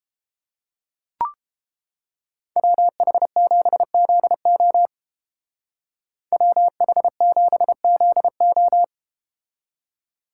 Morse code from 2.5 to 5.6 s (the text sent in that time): WH7ZO